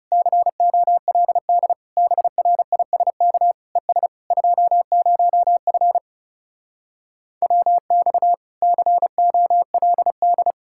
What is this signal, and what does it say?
Morse code, heard as a keyed tone, about 40 wpm, 700 Hz. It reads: COLD BRISK ES 20F WX COLD